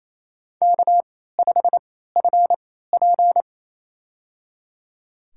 Morse code, heard as a keyed tone, about 28 wpm, spaced as for 16 wpm, 700 Hz. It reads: K5FP
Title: USA Call Signs - 3x Character Spacing - 28wpm